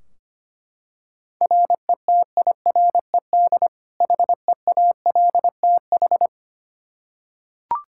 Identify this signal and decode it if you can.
Morse code, heard as a keyed tone, about 25 wpm, 700 Hz. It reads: RETIRED HEALTH
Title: Practice QSO - Exchange 3 - More Info - 25wpm